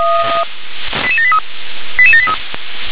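Five-tone selective call (ZVEI), heard as quick runs of five tones, tones of about 70 ms. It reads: 88783 88783
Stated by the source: ZVEI five-tone selective call received on an SDR.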